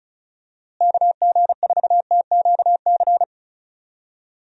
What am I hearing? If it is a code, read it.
Morse code, heard as a keyed tone, about 36 wpm, 700 Hz. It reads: KG4TQC
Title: USA Call Signs - 6 Characters - Repeated 2x - 35wpm